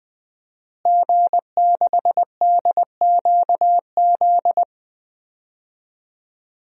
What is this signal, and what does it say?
Morse code, heard as a keyed tone, about 20 wpm, 700 Hz. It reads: G6DQZ
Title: DXCC-100 Call Signs - 5 Characters - Repeated 2x - 20wpm